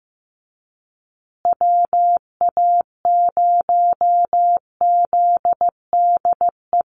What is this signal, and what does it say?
Morse code, heard as a keyed tone, about 15 wpm, 700 Hz. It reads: WA0ZDE
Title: USA Call Signs Repeated 3x 15wpm